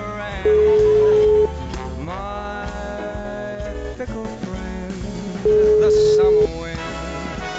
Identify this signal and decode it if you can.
Ringback tone (Europe, 425 Hz), with speech.